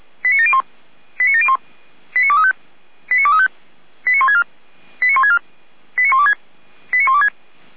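Five-tone selective call (ZVEI), heard as quick runs of five tones, tones of about 70 ms. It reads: 89701 89701 89235 89235 89164 89164 89126 89126
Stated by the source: ZVEI five-tone selective call received on SDR.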